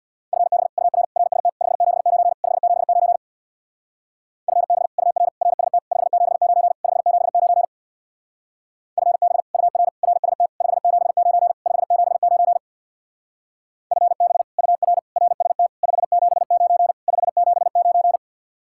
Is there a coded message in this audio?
Yes: Morse code from 0.3 to 18.2 s, heard as a keyed tone, about 63 wpm, 700 Hz. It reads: FB UR RST 579 579 FB UR RST 579 579 FB UR RST 579 579 FB UR RST 579 579